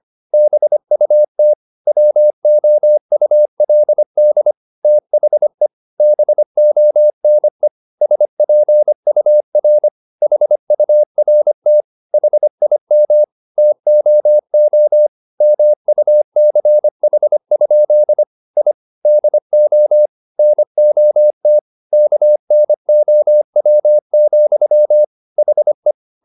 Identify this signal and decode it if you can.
Morse code, heard as a keyed tone, about 25 wpm, 600 Hz. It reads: BUT WOULD THE BONE SPUR HURT HIM TOO MUCH? I DO NOT KNOW, HE
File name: the old man and the sea part 1 25wpm